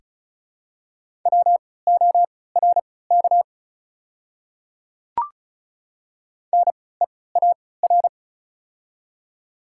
Morse code, heard as a keyed tone, about 34 wpm, spaced as for 20 wpm, 700 Hz. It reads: WORK NEAR
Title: 3x Character Spacing - Top 200 Words - 35wpm